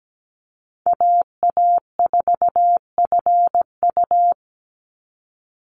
Morse code, heard as a keyed tone, about 17 wpm, 700 Hz. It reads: AA4FU
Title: Call Signs Repeated 2x - 17wpm